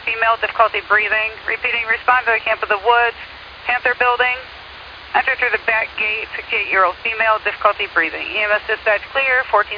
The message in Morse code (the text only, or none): none